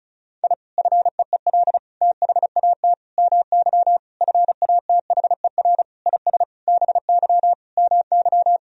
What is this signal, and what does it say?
Morse code, heard as a keyed tone, about 36 wpm, 700 Hz. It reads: I FEEL THAT MY FATHER IS BY MY
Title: Sentences from Top 1000 Words - Review Part 1 - 35wpm